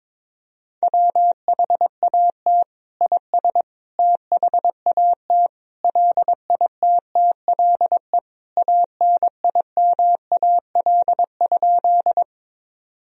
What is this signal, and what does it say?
Morse code, heard as a keyed tone, about 22 wpm, 700 Hz. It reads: WHAT IS THAT LITTLE ANIMAL?
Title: Sentences from Top 1000 Words - Review Part 1 - 22wpm